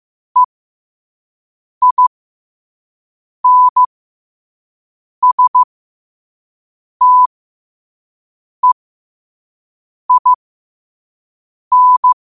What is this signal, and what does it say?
Morse code, heard as a keyed tone, about 15 wpm, spaced as for 5 wpm, 1000 Hz. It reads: EINSTEIN
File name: 5wpm